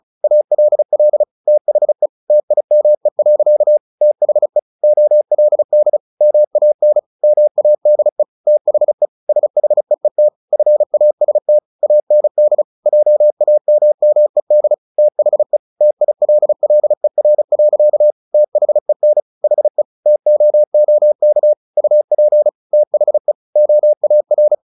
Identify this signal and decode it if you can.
Morse code, heard as a keyed tone, about 34 wpm, 600 Hz. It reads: ALL THE TIME. THE OLD MAN MADE THE SHEET FAST AND JAMMED THE TILLER. THEN HE TOOK UP THE OAR
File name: the old man and the sea 2 35wpm